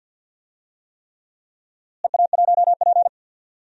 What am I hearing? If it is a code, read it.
Morse code, heard as a keyed tone, about 47 wpm, 700 Hz. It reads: EA1P
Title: DXCC-100 Call Signs - 4 Characters - Repeated 3x - 50wpm